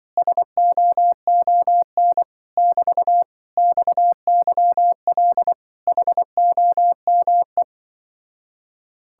Morse code, heard as a keyed tone, about 24 wpm, 700 Hz. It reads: SOON <BT> XYL HOME
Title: Speed-Racing Common QSO Phrases Encore - 20wpm